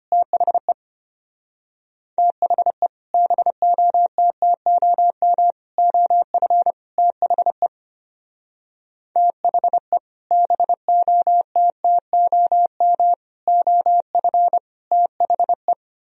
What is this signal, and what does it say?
Morse code, heard as a keyed tone, about 31 wpm, 700 Hz. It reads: THE THE BOTTOM OF THE THE BOTTOM OF THE
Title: Speed-Racing - Sets of 4 Words 25wpm